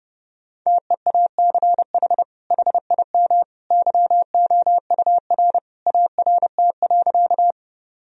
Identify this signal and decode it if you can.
Morse code, heard as a keyed tone, about 32 wpm, 700 Hz. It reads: TEACH HIM YOUR ART.
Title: Sentences from Top 1000 Words - Review Part 3 - 30wpm